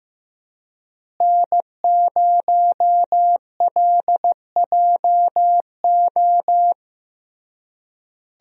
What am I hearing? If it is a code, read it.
Morse code, heard as a keyed tone, about 15 wpm, 700 Hz. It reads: N0LJO
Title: USA Call Signs - 5 Characters - Repeated 3x - 15wpm